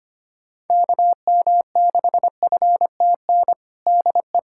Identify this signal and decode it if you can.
Morse code, heard as a keyed tone, about 25 wpm, 700 Hz. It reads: KM6FTN DE
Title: USA Callsign DE Callsign 25wpm